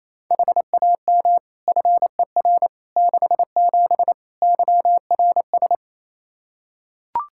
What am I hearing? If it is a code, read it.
Morse code, heard as a keyed tone, about 28 wpm, 700 Hz. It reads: HAM FER 67 YRS
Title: Practice QSO - Exchange 3 - More Info - 28wpm